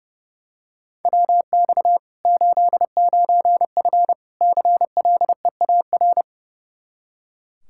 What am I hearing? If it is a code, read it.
Morse code, heard as a keyed tone, about 30 wpm, 700 Hz. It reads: WX 89F CLEAR
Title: Practice QSO - Exchange 2 - Info - 30wpm